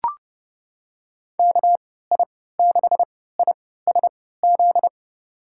Morse code, heard as a keyed tone, about 30 wpm, spaced as for 17 wpm, 700 Hz. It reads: KI6ISZ
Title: USA Call Signs - 3x Character Spacing - 30wpm